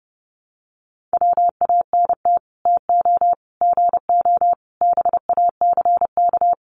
Morse code, heard as a keyed tone, about 30 wpm, 700 Hz. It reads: WANT TO GO BACK
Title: Sets of 5 Words 30wpm